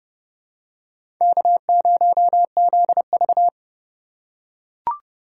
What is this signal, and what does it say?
Morse code, heard as a keyed tone, about 30 wpm, 700 Hz. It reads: K0ZV
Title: USA Call Signs - 4 Characters - 3x Character Spacing - 30wpm